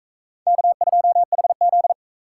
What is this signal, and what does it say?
Morse code, heard as a keyed tone, about 36 wpm, 700 Hz. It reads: K2HZ